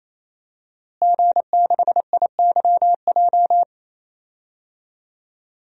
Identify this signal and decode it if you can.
Morse code, heard as a keyed tone, about 28 wpm, 700 Hz. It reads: G6IYJ
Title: DXCC-100 Call Signs - 5 Characters - Repeated 3x - 28wpm